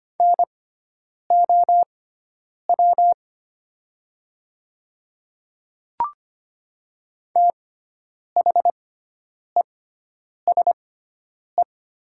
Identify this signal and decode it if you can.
Morse code, heard as a keyed tone, about 25 wpm, spaced as for 9 wpm, 700 Hz. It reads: NOW THESE
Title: 6x Character Spacing - Top 100 Words - 25wpm